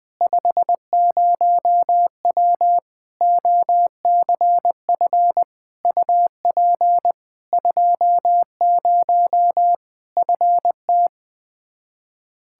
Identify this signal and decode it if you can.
Morse code, heard as a keyed tone, about 21 wpm, 700 Hz. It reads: 50W OCF UP 20 FT
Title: Practice QSO - Exchange 2 - Info - 20wpm